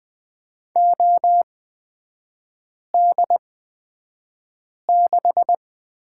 Morse code, heard as a keyed tone, about 20 wpm, spaced as for 5 wpm, 700 Hz. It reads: OD6
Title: 250 Lesson 40e - BK - Three Characters 20wpm